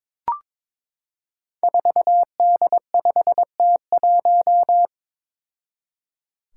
Morse code, heard as a keyed tone, about 22 wpm, 700 Hz. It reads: 4D5T1
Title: Five Challenging Letter-Number Combinations 22wpm